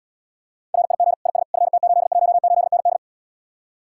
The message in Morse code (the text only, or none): RIG IS FT818ND